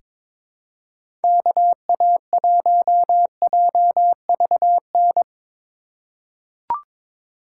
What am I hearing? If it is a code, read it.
Morse code, heard as a keyed tone, about 23 wpm, 700 Hz. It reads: KA1JVN
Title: USA Call Signs - 6 Characters - 4x Character Spacing - 22wpm